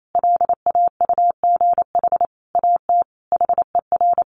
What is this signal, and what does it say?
Morse code, heard as a keyed tone, about 28 wpm, 700 Hz. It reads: LAUGH AT HER
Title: Sentences from Top 600 Words 28wpm